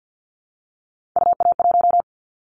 Morse code, heard as a keyed tone, about 50 wpm, 700 Hz. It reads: UA1
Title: Callsign Prefixes for Top 25 DXCC - 50wpm